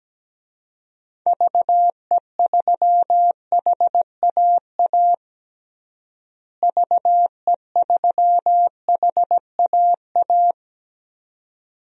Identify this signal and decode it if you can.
Morse code, heard as a keyed tone, about 17 wpm, 700 Hz. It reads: VE3HAA VE3HAA
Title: DXCC-100 Call Signs - 6 Characters - Repeated 2x - 17wpm